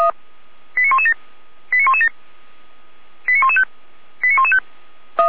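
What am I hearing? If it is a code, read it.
Five-tone selective call (ZVEI), heard as quick runs of five tones, tones of about 70 ms. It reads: 89117 89117 89115 89115
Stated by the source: ZVEI five-tone selective call received on an SDR.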